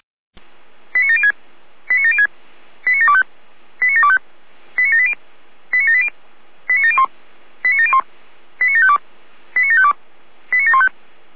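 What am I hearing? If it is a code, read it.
Five-tone selective call (ZVEI), heard as quick runs of five tones, tones of about 70 ms. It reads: 89706 89706 89725 89725 89780 89780 89701 89701 89752 89752 89715
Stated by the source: ZVEI five-tone selective call received by SDR.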